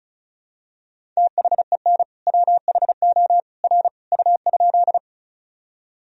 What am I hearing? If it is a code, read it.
Morse code, heard as a keyed tone, about 34 wpm, 700 Hz. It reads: THEN WHO R U?
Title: Sentences from Top 100 Words and Abbreviations 35wpm